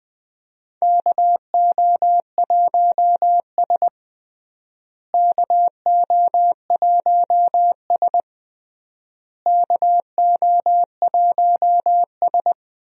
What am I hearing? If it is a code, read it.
Morse code, heard as a keyed tone, about 21 wpm, 700 Hz. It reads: KO1S KO1S KO1S
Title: USA Call Signs - 4 Characters - Repeated 3x - 20wpm